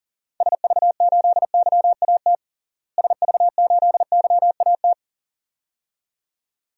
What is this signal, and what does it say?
Morse code, heard as a keyed tone, about 40 wpm, 700 Hz. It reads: SV8YAT SV8YAT
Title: DXCC-100 Call Signs - 6 Characters - Repeated 2x - 40wpm